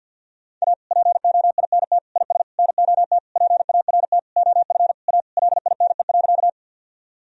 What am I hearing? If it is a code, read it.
Morse code, heard as a keyed tone, about 52 wpm, 700 Hz. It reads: A POINT IS NOT PART OF A LINE.
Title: Sentences from Top 1000 Words - Review Part 1 - 50wpm